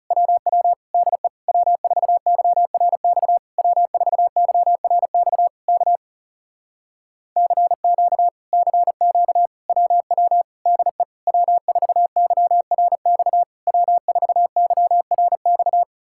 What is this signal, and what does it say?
Morse code, heard as a keyed tone, about 41 wpm, 700 Hz. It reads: WW DE W4YRX W4YRX K CQ CQ WW DE W4YRX W4YRX